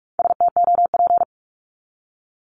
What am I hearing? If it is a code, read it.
Morse code, heard as a keyed tone, about 44 wpm, 700 Hz. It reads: STOP
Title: Top 500 Words 45wpm